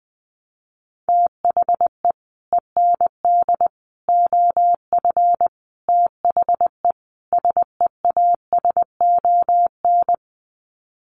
Morse code, heard as a keyed tone, about 20 wpm, 700 Hz. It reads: THE END OF THE SEASON
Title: Sets of 5 Words Encore 20wpm